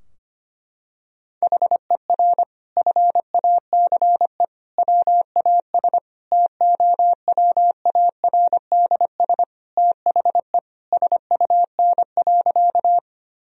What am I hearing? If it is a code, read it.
Morse code, heard as a keyed tone, about 25 wpm, 700 Hz. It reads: HER FACE WAS TOWARDS THE SUN.
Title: Sentences from Top 1000 Words - Review Part 2 - 25wpm